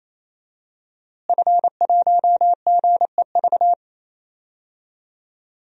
Morse code, heard as a keyed tone, about 28 wpm, 700 Hz. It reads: F1GEV
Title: DXCC-100 Call Signs - 5 Characters - Repeated 3x - 28wpm